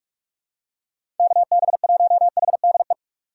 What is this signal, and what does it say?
Morse code, heard as a keyed tone, about 43 wpm, 700 Hz. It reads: KB1HDE
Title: USA Call Signs - 6 Characters - 2x Character Spacing - 45wpm